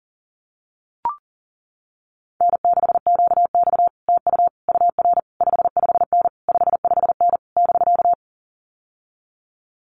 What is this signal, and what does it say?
Morse code, heard as a keyed tone, about 40 wpm, 700 Hz. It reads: N6QX TU UR 55N 55N <BK>
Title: Common QSO Phrases Encore 40wpm